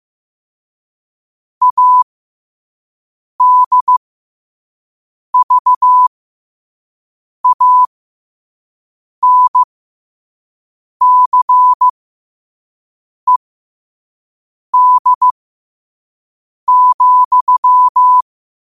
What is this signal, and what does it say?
Morse code, heard as a keyed tone, about 15 wpm, spaced as for 5 wpm, 1000 Hz. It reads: ADVANCED,
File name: qotdfiles / 5wpm